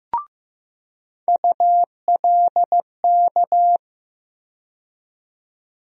Morse code, heard as a keyed tone, about 15 wpm, 700 Hz. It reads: ULK
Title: USA Call Sign Postfixes - 3 Letters 15wpm